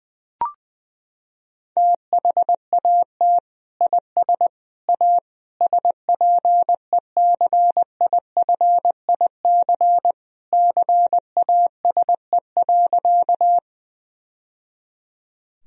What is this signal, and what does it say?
Morse code, heard as a keyed tone, about 21 wpm, 700 Hz. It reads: THAT IS A SPECIFIC CASE.
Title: Sentences from Top 1000 Words - Review Part 3 - 20wpm